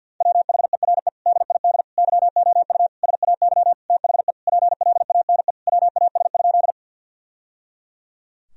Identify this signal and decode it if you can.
Morse code, heard as a keyed tone, about 50 wpm, 700 Hz. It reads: WHERE DID YOU SAY THE PLANE WAS?
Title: Sentences from Top 1000 Words - Review Part 3 - 50wpm